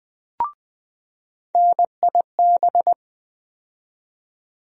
Morse code, heard as a keyed tone, about 20 wpm, 700 Hz. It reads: NIB
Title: USA Call Sign Postfixes - 3 Letters 20wpm